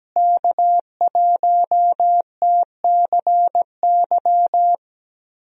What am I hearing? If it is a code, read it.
Morse code, heard as a keyed tone, about 17 wpm, 700 Hz. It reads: K1TCY
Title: USA Call Signs - 5 Characters - Repeated 3x - 17wpm